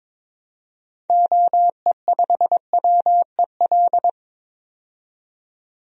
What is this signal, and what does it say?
Morse code, heard as a keyed tone, about 22 wpm, 700 Hz. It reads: OE5WEL
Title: DXCC-100 Call Signs - 6 Characters - 22wpm